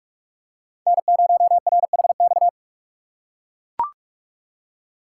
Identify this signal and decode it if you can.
Morse code, heard as a keyed tone, about 43 wpm, 700 Hz. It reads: N0RHX
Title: USA Call Signs - 5 Characters - 3x Character Spacing - 45wpm